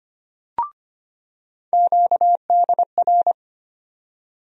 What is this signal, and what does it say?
Morse code, heard as a keyed tone, about 25 wpm, 700 Hz. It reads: QDR
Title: USA Call Sign Postfixes - 3 Letters 25wpm